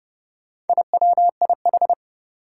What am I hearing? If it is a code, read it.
Morse code, heard as a keyed tone, about 30 wpm, 700 Hz. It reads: IWIH
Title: Four Challenging Letter-Number Combinations 30wpm